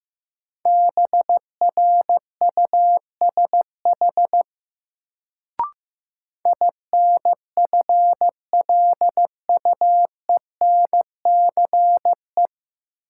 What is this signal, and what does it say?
Morse code, heard as a keyed tone, about 15 wpm, 700 Hz. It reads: BRUSH INFLUENCE
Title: Sentences from Top 1100 Words - New Word Drill Alt - 15wpm